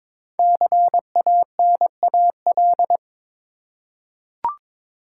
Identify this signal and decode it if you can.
Morse code, heard as a keyed tone, about 23 wpm, 700 Hz. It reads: CANAL
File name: 5 Letter Words - First 3 Letters Given - 22wpm